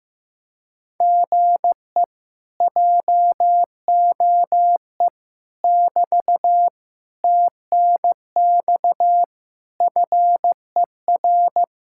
Morse code, heard as a keyed tone, about 15 wpm, 700 Hz. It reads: GE JOE <BT> TNX FER
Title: Speed-Racing Common QSO Phrases Encore - 15wpm